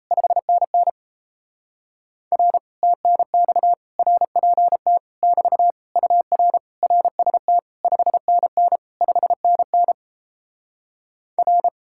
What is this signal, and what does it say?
Morse code, heard as a keyed tone, about 38 wpm, 700 Hz. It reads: 5NN R TNX RPT <BT> UR RST 5NN 5NN R